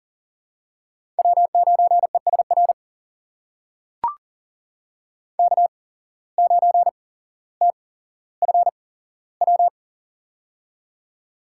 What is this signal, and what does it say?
Morse code, heard as a keyed tone, about 40 wpm, spaced as for 17 wpm, 700 Hz. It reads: W9ESR K9TFW
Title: USA Call Signs - 5 Characters - 8x Character Spacing - 40wpm